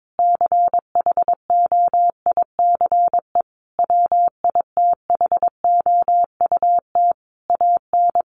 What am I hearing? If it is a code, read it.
Morse code, heard as a keyed tone, about 22 wpm, 700 Hz. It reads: CHOICE WITHOUT AN
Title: Sentences from Top 1500 Words 22wpm